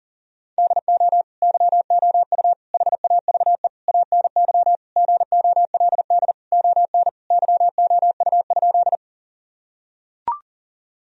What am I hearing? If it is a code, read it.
Morse code, heard as a keyed tone, about 40 wpm, 700 Hz. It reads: DO YOU HAVE ANY GOLD ON YOU?
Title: Sentences from Top 1000 Words - Review Part 3 - 40wpm